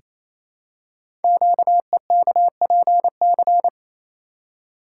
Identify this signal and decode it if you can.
Morse code, heard as a keyed tone, about 29 wpm, 700 Hz. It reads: QEKPC
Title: Code Groups - 5 Letters - 28wpm